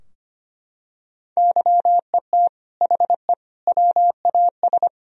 Morse code, heard as a keyed tone, about 25 wpm, 700 Hz. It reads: YET HE WAS
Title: Sentences from Top 1000 Words - Review Part 3 - 25wpm